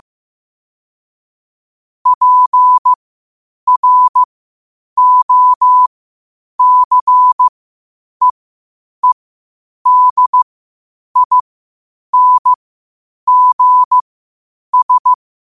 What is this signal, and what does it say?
Morse code, heard as a keyed tone, about 15 wpm, spaced as for 8 wpm, 1000 Hz. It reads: PROCEEDINGS
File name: qotdfiles / 8wpm